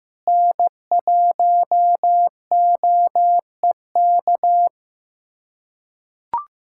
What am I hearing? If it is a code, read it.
Morse code, heard as a keyed tone, about 15 wpm, 700 Hz. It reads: N1OEK
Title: USA Callsign DE Callsign - 5 Characters - 15wpm